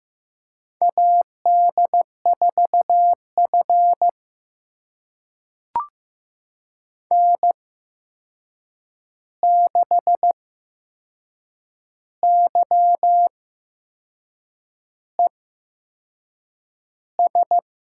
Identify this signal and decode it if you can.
Morse code, heard as a keyed tone, about 15 wpm, spaced as for 6 wpm, 700 Hz. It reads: AD4F N6YES
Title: USA Call Signs - 8x Character Spacing - 15wpm